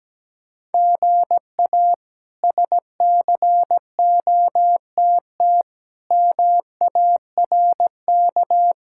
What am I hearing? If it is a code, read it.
Morse code, heard as a keyed tone, about 17 wpm, 700 Hz. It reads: GA SCOTT MARK
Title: K1USN SST QSO Practice - 17wpm